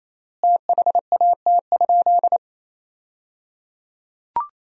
Morse code, heard as a keyed tone, about 28 wpm, 700 Hz. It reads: THAT?
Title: 3x Word-Spacing - Sentences from Top 100 Words - 28wpm